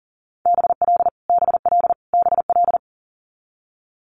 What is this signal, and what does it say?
Morse code, heard as a keyed tone, about 38 wpm, 690 Hz. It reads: BL BL BL
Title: Multi-letter Phonemes Repeated 3x 40wpm